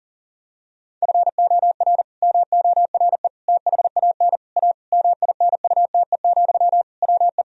Morse code, heard as a keyed tone, about 40 wpm, 700 Hz. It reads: FOR MORE THAN A MINUTE, WE
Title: Sentences from Top 1000 Words - Review Part 2 - 40wpm